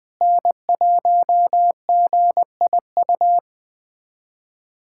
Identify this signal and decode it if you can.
Morse code, heard as a keyed tone, about 20 wpm, 700 Hz. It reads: N1GIU
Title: USA Callsign DE Callsign - 5 Characters - 20wpm